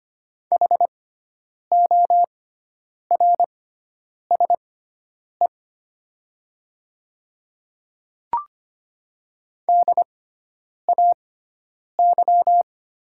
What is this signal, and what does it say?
Morse code, heard as a keyed tone, about 25 wpm, spaced as for 9 wpm, 700 Hz. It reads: HORSE DAY
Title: 6x Character Spacing - Top 500 Words - 25wpm